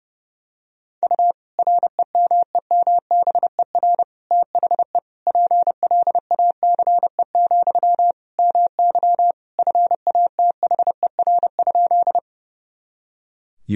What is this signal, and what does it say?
Morse code, heard as a keyed tone, about 30 wpm, 700 Hz. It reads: U REMEMBER THE PLACE, MY FATHER?
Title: Sentences from Top 400 Words and Abbreviations 30wpm